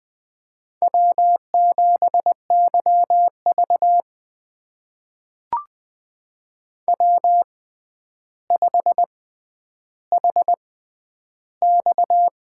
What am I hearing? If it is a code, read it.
Morse code, heard as a keyed tone, about 20 wpm, spaced as for 10 wpm, 700 Hz. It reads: W7YV W5HX
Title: USA Call Signs - 4 Characters - 6x Character Spacing - 20wpm